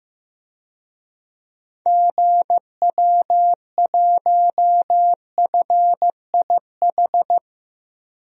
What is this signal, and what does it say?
Morse code, heard as a keyed tone, about 15 wpm, 700 Hz. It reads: GW1FIH
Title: DXCC-100 Call Signs - 6 Characters - Repeated 3x - 15wpm